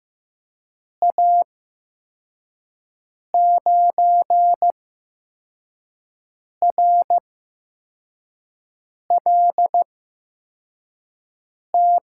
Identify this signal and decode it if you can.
Morse code, heard as a keyed tone, about 15 wpm, spaced as for 4 wpm, 700 Hz. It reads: A9RLT
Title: USA Call Signs - 6 Characters - 8x Character Spacing - 15wpm